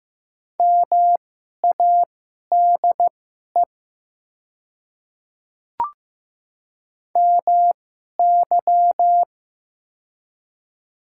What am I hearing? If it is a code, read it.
Morse code, heard as a keyed tone, about 15 wpm, spaced as for 11 wpm, 700 Hz. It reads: MADE MY
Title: 2x Character Spacing - Top 200 Words - 15wpm